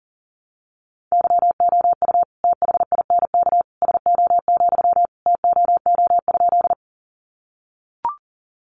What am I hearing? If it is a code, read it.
Morse code, heard as a keyed tone, about 40 wpm, 700 Hz. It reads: YOU THINK SO, TOO?
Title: Sentences from Top 200 Words 40wpm